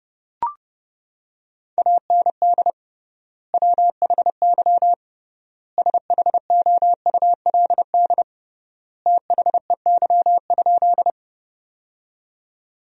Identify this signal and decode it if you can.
Morse code, heard as a keyed tone, about 30 wpm, 700 Hz. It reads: AND WHY SHOULD THEY?
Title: 3x Word-Spacing - Sentences from Top 300 Words - 30wpm